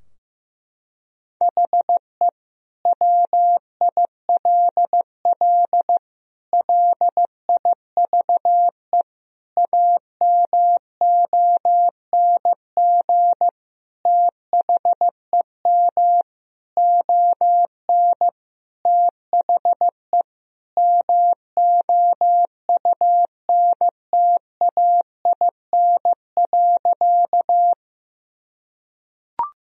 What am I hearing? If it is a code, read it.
Morse code, heard as a keyed tone, about 15 wpm, 700 Hz. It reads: HE WILL LIVE AMONG THEM ON THE MOUNTAIN.
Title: Sentences from Top 1000 Words - Review Part 3 - 15wpm